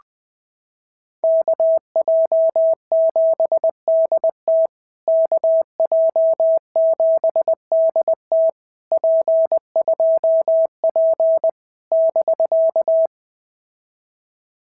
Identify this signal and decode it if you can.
Morse code, heard as a keyed tone, about 23 wpm, 650 Hz. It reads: KJ7DT KJ7DT P2P <BK>
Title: POTA QSO Practice - 20wpm